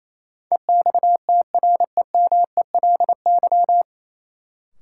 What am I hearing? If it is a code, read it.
Morse code, heard as a keyed tone, about 28 wpm, 700 Hz. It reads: EXTREMELY
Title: Sentences from Top 1000 Words - Review Part 7 - New Word Drill - 28wpm